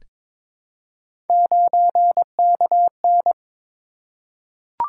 Morse code, heard as a keyed tone, about 22 wpm, 700 Hz. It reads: 9KN
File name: Three Letter-Number Combinations - 22wpm